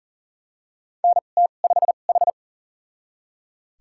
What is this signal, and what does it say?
Morse code, heard as a keyed tone, about 40 wpm, spaced as for 29 wpm, 700 Hz. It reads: NT5H
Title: USA Call Signs - 4 Characters - 2x Character Spacing - 40wpm